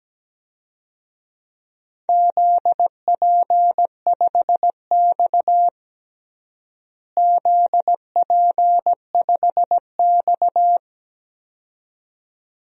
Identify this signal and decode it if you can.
Morse code, heard as a keyed tone, about 17 wpm, 700 Hz. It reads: ZP5X ZP5X
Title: DXCC-100 Call Signs - 4 Characters - Repeated 2x - 17wpm